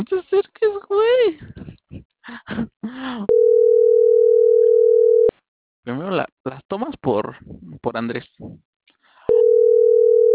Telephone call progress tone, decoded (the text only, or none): ringback tone